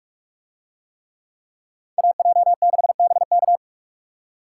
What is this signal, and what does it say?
Morse code, heard as a keyed tone, about 44 wpm, 700 Hz. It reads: AJ6BK